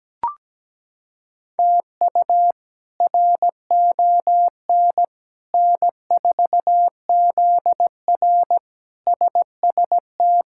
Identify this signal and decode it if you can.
Morse code, heard as a keyed tone, about 17 wpm, 700 Hz. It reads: TU RON N4ZR SST
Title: K1USN SST QSO Practice - 17wpm